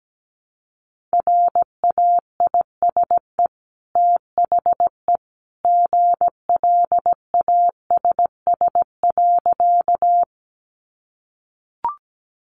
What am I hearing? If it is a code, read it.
Morse code, heard as a keyed tone, about 17 wpm, 700 Hz. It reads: RAISE THE GLASS.
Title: Sentences from Top 1400 Words 17wpm